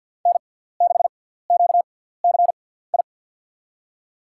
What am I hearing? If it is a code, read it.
Morse code, heard as a keyed tone, about 52 wpm, spaced as for 17 wpm, 700 Hz. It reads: N6QCI
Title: USA Call Signs - 5 Characters - 6x Character Spacing - 50wpm